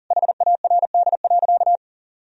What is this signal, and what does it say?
Morse code, heard as a keyed tone, about 40 wpm, 700 Hz. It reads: HARD.
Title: Sentences from Top 1000 Words - Review Part 4 - 40wpm